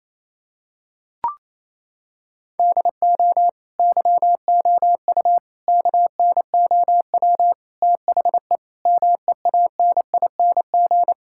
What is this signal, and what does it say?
Morse code, heard as a keyed tone, about 27 wpm, 700 Hz. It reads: DO YOU KNOW THE MEANING
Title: Sentences from Top 1000 Words - Review Part 6 - 28wpm